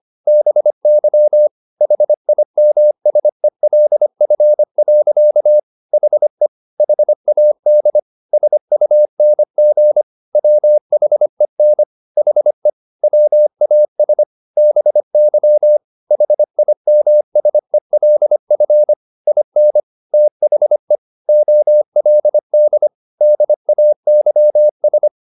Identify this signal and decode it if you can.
Morse code, heard as a keyed tone, about 25 wpm, 600 Hz. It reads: BY HIMSELF. HE HAD SUNG WHEN HE WAS BY HIMSELF IN THE OLD DAYS
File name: the old man and the sea part 1 25wpm